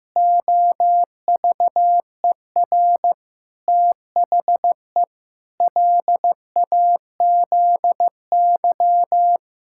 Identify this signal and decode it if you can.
Morse code, heard as a keyed tone, about 16 wpm, 700 Hz. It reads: OVER THE LAZY
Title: Pangram Challenge v2 - 15wpm